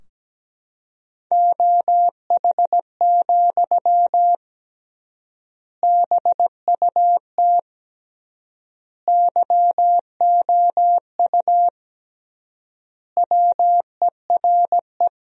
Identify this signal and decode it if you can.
Morse code, heard as a keyed tone, about 17 wpm, 700 Hz. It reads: OH, BUT YOU WERE
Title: 3x Word-Spacing - Sentences from Top 500 Words - 17wpm